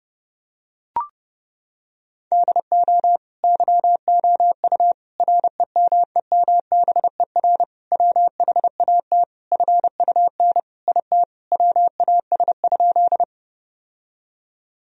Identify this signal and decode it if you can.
Morse code, heard as a keyed tone, about 32 wpm, 700 Hz. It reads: DO YOU REMEMBER WHAT FUN IT WAS?
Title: Sentences from Top 1000 Words - Review Part 4 - 30wpm